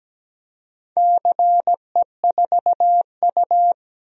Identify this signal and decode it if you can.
Morse code, heard as a keyed tone, about 17 wpm, 700 Hz. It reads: CE4U
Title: DXCC-100 Call Signs - 4 Characters - Repeated 2x - 17wpm